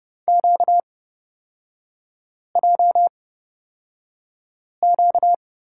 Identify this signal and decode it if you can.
Morse code, heard as a keyed tone, about 29 wpm, spaced as for 5 wpm, 700 Hz. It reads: QJQ
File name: Lesson 38e - Letter q - Three Characters 30wpm